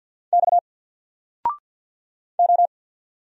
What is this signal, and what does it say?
Morse code, heard as a keyed tone, about 48 wpm, 700 Hz. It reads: X O
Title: Single Letter-Number Alt 50wpm